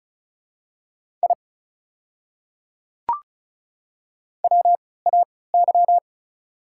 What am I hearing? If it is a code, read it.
Morse code, heard as a keyed tone, about 36 wpm, spaced as for 20 wpm, 690 Hz. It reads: I WAY
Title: 3x Character Spacing - Top 100 Words - 35wpm